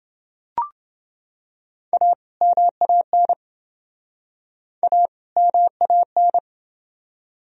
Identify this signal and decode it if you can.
Morse code, heard as a keyed tone, about 27 wpm, 700 Hz. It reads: A MAN A MAN